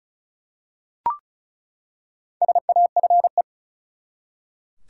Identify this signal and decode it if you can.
Morse code, heard as a keyed tone, about 34 wpm, 700 Hz. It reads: SAFE
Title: Four Letter Words - 35wpm